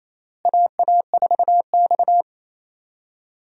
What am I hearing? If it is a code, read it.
Morse code, heard as a keyed tone, about 28 wpm, 700 Hz. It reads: AA4X
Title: USA Call Signs - 4 Characters - Repeated 3x - 28wpm